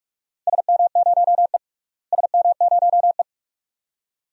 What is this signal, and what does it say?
Morse code, heard as a keyed tone, about 46 wpm, 700 Hz. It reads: SM0E SM0E
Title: DXCC-100 Call Signs - 4 Characters - Repeated 2x - 45wpm